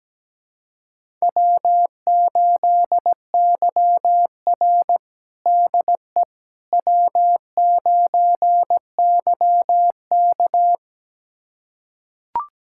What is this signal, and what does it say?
Morse code, heard as a keyed tone, about 17 wpm, 700 Hz. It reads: W8YR DE W9YK
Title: USA Callsign DE Callsign - 4 Characters - 17wpm